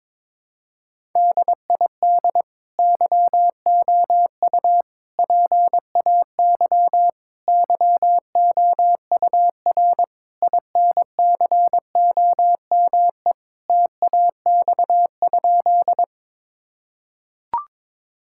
Morse code, heard as a keyed tone, about 22 wpm, 700 Hz. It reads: DID YOU PAY YOUR INCOME TAX?
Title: Sentences from Top 1000 Words - Review Part 5 - 22wpm